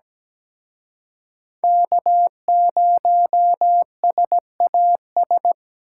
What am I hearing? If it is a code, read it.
Morse code, heard as a keyed tone, about 20 wpm, 700 Hz. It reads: K0SAS
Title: Speed Racing - USA Call Signs 17wpm